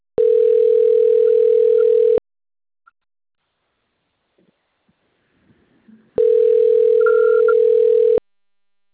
Ringback tone (North America), 440 + 480 Hz.